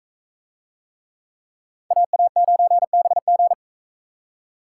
Morse code, heard as a keyed tone, about 43 wpm, 700 Hz. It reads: AA9BG